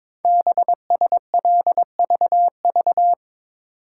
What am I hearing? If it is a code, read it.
Morse code, heard as a keyed tone, about 22 wpm, 700 Hz. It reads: BSLVV
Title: Code Groups - 5 Letters - 22wpm